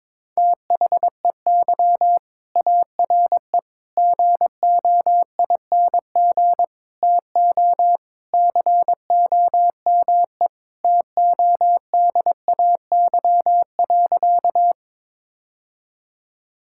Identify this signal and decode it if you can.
Morse code, heard as a keyed tone, about 22 wpm, 700 Hz. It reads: THEY ARE GOING TO COME TODAY.
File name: practice / 22wpm / Sentences with 3-Word Phrases - 22wpm